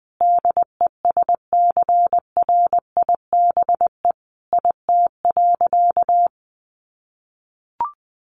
Morse code, heard as a keyed tone, about 20 wpm, 700 Hz. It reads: DESCRIBE IT.
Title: Sentences from Top 1100 Words 20wpm